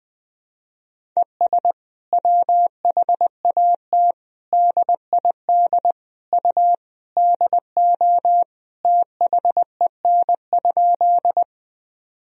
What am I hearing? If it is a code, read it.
Morse code, heard as a keyed tone, about 21 wpm, 700 Hz. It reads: ES WHAT DID U DO THEN?
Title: Sentences from Top 100 Words and Abbreviations 20wpm